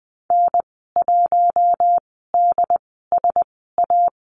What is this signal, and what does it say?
Morse code, heard as a keyed tone, about 20 wpm, spaced as for 14 wpm, 700 Hz. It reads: N1DSA
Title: 069 Lesson 12g - Letter d - Call Signs 20wpm